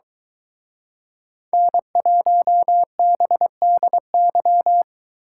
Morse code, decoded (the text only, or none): N1BDY